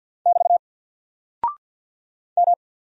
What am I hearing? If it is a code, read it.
Morse code, heard as a keyed tone, about 48 wpm, 700 Hz. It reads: <BT> M
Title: Single Character (Letter:Number:Punctuation:Prosign) Alt 50wpm